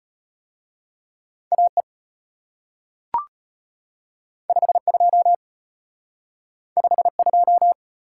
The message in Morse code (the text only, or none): AE 52 52